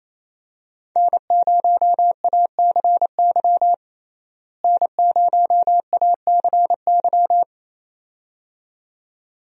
Morse code, heard as a keyed tone, about 28 wpm, 700 Hz. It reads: N0ACY N0ACY
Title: USA Call Signs - 5 Characters - Repeated 2x - 28wpm